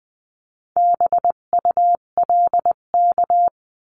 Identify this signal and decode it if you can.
Morse code, heard as a keyed tone, about 20 wpm, 700 Hz. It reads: BULK
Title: 258 Lesson 42c - Words 16-20wpm